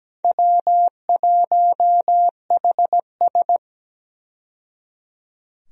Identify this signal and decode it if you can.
Morse code, heard as a keyed tone, about 17 wpm, 700 Hz. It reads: W1HS
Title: USA Call Signs - 4 Characters - 17wpm